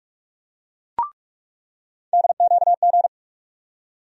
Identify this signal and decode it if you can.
Morse code, heard as a keyed tone, about 42 wpm, 700 Hz. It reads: DQG
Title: USA Call Sign Postfixes - 3 Letters 45wpm